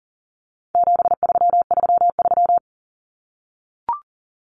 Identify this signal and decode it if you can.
Morse code, heard as a keyed tone, about 38 wpm, 700 Hz. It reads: 7333
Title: Numbers - 4-Digit 40wpm